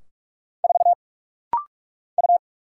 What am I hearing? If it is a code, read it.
Morse code, heard as a keyed tone, about 44 wpm, 690 Hz. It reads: 4 U
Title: Single Letter-Number - Rapid-Fire 45wpm